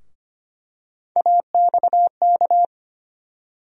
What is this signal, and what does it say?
Morse code, heard as a keyed tone, about 25 wpm, 700 Hz. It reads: AXK
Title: USA Call Sign Postfixes - 3 Letters 25wpm